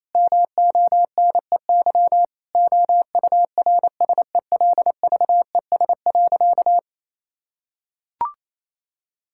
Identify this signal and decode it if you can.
Morse code, heard as a keyed tone, about 28 wpm, 700 Hz. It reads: MONEY OURSELVES.
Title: Sentences from Top 1600 Words 28wpm